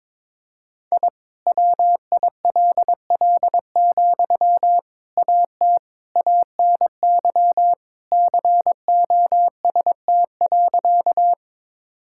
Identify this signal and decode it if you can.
Morse code, heard as a keyed tone, about 21 wpm, 700 Hz. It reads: I WILL, AT ANY COST.
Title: Sentences from Top 1000 Words - Review Part 5 - 22wpm